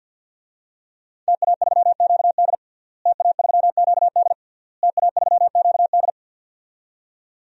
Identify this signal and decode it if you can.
Morse code, heard as a keyed tone, about 46 wpm, 700 Hz. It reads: TA3QD TA3QD TA3QD
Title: DXCC-100 Call Signs - 5 Characters - Repeated 3x - 50wpm